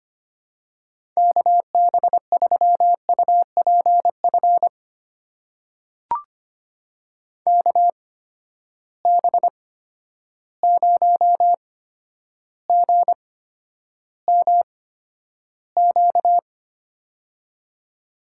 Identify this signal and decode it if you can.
Morse code, heard as a keyed tone, about 25 wpm, spaced as for 11 wpm, 700 Hz. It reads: KB3UPF KB0GMQ
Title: USA Call Signs - 6 Characters - 8x Character Spacing - 25wpm